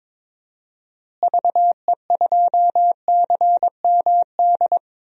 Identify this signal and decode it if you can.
Morse code, heard as a keyed tone, about 21 wpm, 700 Hz. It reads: VE2CMD
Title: DXCC-100 Call Signs - 6 Characters - Repeated 3x - 22wpm